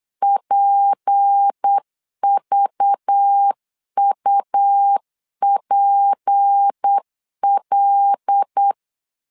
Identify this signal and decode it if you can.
Morse code, heard as a keyed tone, about 8 wpm, 800 Hz. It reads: PVUPL
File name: LEZIONI / 9 28 8wpm 3512